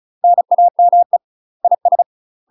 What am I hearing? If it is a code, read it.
Morse code, heard as a keyed tone, about 34 wpm, 700 Hz. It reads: NAME IS
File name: UK MASTER QSOs TRAINING / 35WPM / UK MASTER QSOs 0042